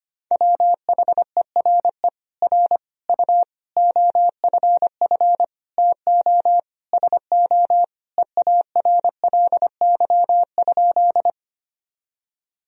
Morse code, heard as a keyed tone, about 25 wpm, 700 Hz. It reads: WHERE R U OFF TO SO EARLY?
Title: Sentences from Top 400 Words and Abbreviations 25wpm